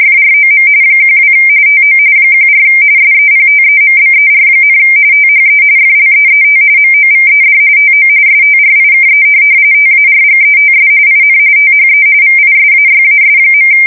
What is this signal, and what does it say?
RTTY, about 45.45 baud. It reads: FROM 12 TO 14 SEPTEMBER, THE FLEET MOVED TO STRIKE TARGETS IN THE VISAYAS.
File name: fsk170 / USS-Massachusetts